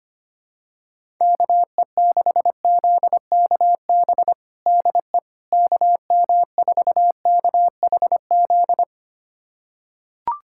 Morse code, heard as a keyed tone, about 25 wpm, 700 Hz. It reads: KE6ZKB DE KM4KHZ
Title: USA Callsign DE Callsign 25wpm